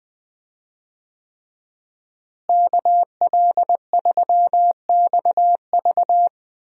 Morse code, heard as a keyed tone, about 21 wpm, 700 Hz. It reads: KL3XV
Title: USA Call Signs - 5 Characters - Repeated 3x - 20wpm